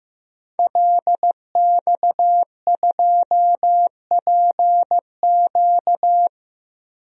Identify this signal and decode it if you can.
Morse code, heard as a keyed tone, about 15 wpm, 700 Hz. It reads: LX2PQ
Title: DXCC-100 Call Signs - 5 Characters - 15wpm